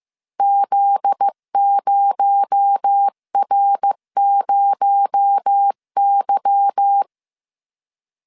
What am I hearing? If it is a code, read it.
Morse code, heard as a keyed tone, about 15 wpm, 800 Hz. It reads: Z0R0Y
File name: CW / DET / 1 9 1500C S A 14wpm 3222